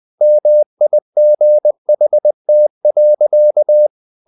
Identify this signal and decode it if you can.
Morse code, heard as a keyed tone, about 20 wpm, 600 Hz. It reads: MIGHT.
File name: the old man and the sea / the old man and the sea part 1 20wpm